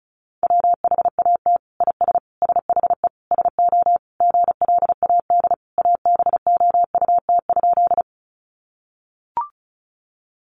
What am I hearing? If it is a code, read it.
Morse code, heard as a keyed tone, about 34 wpm, 700 Hz. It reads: WHAT IS SHE SO GLAD ABOUT?
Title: Sentences from Top 800 Words 35wpm